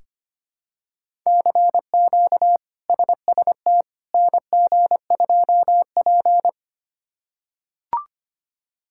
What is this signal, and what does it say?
Morse code, heard as a keyed tone, about 25 wpm, 700 Hz. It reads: CQ SST NG2P
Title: K1USN SST QSO Practice - 25wpm